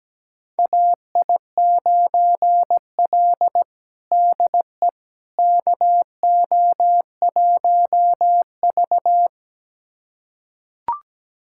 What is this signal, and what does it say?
Morse code, heard as a keyed tone, about 17 wpm, 700 Hz. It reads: AI9L DE KO1V
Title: USA Callsign DE Callsign - 4 Characters - 17wpm